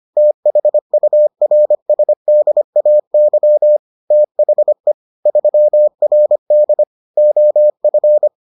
Morse code, heard as a keyed tone, about 25 wpm, 600 Hz. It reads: THURSDAY THE 3RD OF